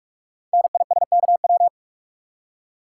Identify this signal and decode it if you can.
Morse code, heard as a keyed tone, about 44 wpm, 690 Hz. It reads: NISKW
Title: Code Groups - 5 Letters - 45wpm